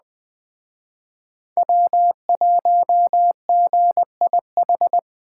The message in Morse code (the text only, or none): W1GIH